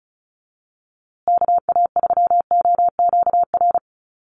Morse code, heard as a keyed tone, about 34 wpm, 700 Hz. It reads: KA3OQR
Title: USA Call Signs Repeated 3x 35wpm